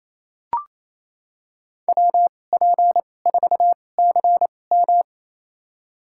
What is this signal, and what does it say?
Morse code, heard as a keyed tone, about 28 wpm, spaced as for 20 wpm, 700 Hz. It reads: WP4CM
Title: USA Call Signs - 5 Characters - 2x Character Spacing - 28wpm